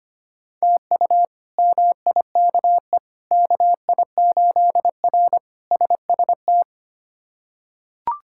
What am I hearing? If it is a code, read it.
Morse code, heard as a keyed tone, about 25 wpm, 700 Hz. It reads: TU MIKE KI8R SST
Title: K1USN SST QSO Practice - 25wpm